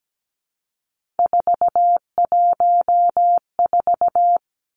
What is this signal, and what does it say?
Morse code, heard as a keyed tone, about 17 wpm, 700 Hz. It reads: 414
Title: Three Challenging Letter-Number Combinations 17wpm